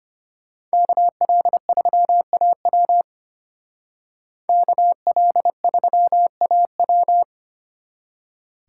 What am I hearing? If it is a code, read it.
Morse code, heard as a keyed tone, about 33 wpm, 700 Hz. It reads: KL3AW KL3AW